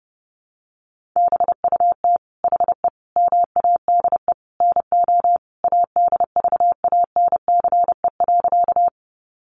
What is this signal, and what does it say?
Morse code, heard as a keyed tone, about 30 wpm, 700 Hz. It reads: BUT HE MADE NO ADVANCE.
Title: Sentences from Top 1400 Words 30wpm